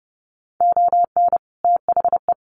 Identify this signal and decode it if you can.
Morse code, heard as a keyed tone, about 30 wpm, 700 Hz. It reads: ON THE
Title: Sets of 3 Words 30wpm